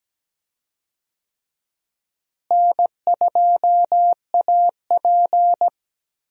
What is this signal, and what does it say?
Morse code, heard as a keyed tone, about 17 wpm, 700 Hz. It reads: N2AP
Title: USA Call Signs - 4 Characters - Repeated 3x - 17wpm